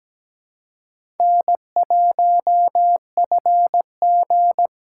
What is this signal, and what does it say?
Morse code, heard as a keyed tone, about 17 wpm, 700 Hz. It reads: N1FG
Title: USA Call Signs - 4 Characters - 3x Character Spacing - 17wpm